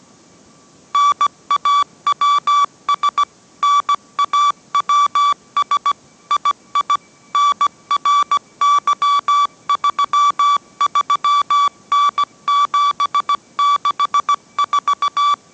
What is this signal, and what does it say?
Morse code, heard as a keyed tone, about 17 wpm, 1150 Hz. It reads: NAWS NAWS II NRY33N764